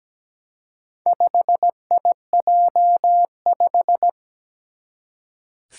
Morse code, heard as a keyed tone, about 17 wpm, 700 Hz. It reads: 5IJ5
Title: Four Challenging Letter-Number Combinations 17wpm